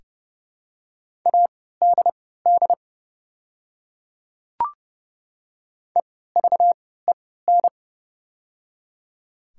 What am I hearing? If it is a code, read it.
Morse code, heard as a keyed tone, about 30 wpm, spaced as for 17 wpm, 700 Hz. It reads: ADD EVEN
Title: 3x Character Spacing - Top 300 Words - 30wpm